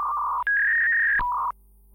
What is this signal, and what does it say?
Fax machine handshake.